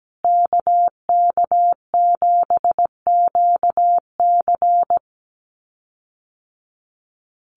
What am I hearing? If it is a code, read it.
Morse code, heard as a keyed tone, about 17 wpm, 700 Hz. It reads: KK7QC
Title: USA Call Signs Repeated 2x 17wpm